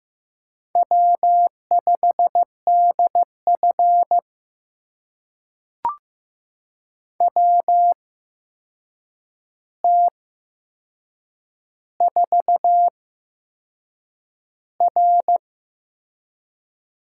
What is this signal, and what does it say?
Morse code, heard as a keyed tone, about 15 wpm, spaced as for 6 wpm, 700 Hz. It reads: W5DF WT4R
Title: USA Call Signs - 4 Characters - 8x Character Spacing - 15wpm